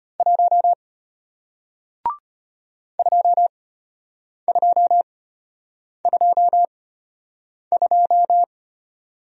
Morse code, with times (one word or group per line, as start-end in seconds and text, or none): 0.2-0.7 1
3.0-8.4 2222